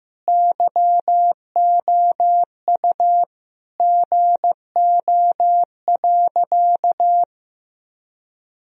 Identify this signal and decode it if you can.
Morse code, heard as a keyed tone, about 16 wpm, 700 Hz. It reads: YOU GO.
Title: Sentences from Top 1000 Words - Review Part 1 - 15wpm